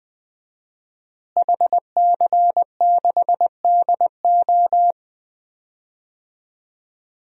Morse code, heard as a keyed tone, about 20 wpm, 700 Hz. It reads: HC6DO
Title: DXCC-100 Call Signs - 5 Characters - Repeated 3x - 20wpm